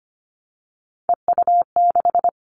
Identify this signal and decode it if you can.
Morse code, heard as a keyed tone, about 25 wpm, 700 Hz. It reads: EU6
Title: Callsign Prefixes for Top 50 DXCC - 25wpm